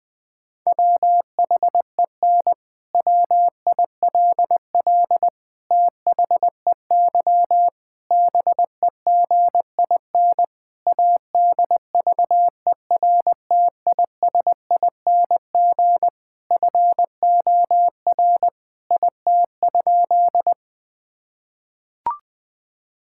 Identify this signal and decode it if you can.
Morse code, heard as a keyed tone, about 20 wpm, 700 Hz. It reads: WHEN WILL THEY BEGIN ADVERTISING FOR IT?
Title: Sentences from Top 1000 Words - Review Part 6 - 20wpm